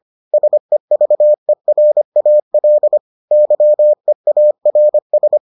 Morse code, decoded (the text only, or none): SEVERAL YEARS